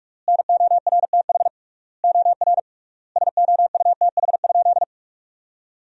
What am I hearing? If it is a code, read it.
Morse code, heard as a keyed tone, about 41 wpm, 700 Hz. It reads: NORTH OR SOUTH?
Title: 3x Word-Spacing - Sentences from Top 400 Words - 45wpm